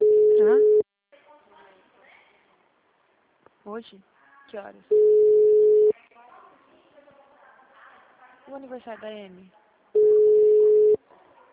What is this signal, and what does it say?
Ringback tone (Europe, 425 Hz), with speech.